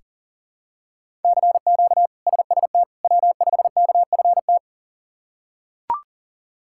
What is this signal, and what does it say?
Morse code, heard as a keyed tone, about 40 wpm, 700 Hz. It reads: CQ SST W5KFT
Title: K1USN SST QSO Practice - 40wpm